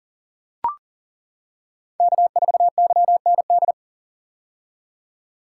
Morse code, heard as a keyed tone, about 40 wpm, 700 Hz. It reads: K4YND
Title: K1USN SST QSO Practice - 40wpm